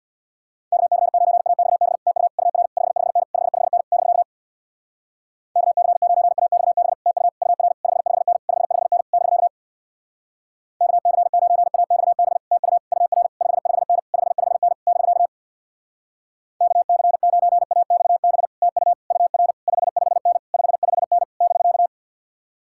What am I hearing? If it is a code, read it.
Morse code, heard as a keyed tone, about 63 wpm, 700 Hz. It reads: KC9AXB TU UR 55N 55N <BK> KC9AXB TU UR 55N 55N <BK> KC9AXB TU UR 55N 55N <BK> KC9AXB TU UR 55N 55N <BK>